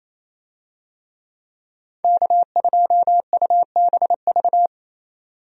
Morse code, heard as a keyed tone, about 28 wpm, 700 Hz. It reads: K2UBV
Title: USA Call Signs - 5 Characters - 28wpm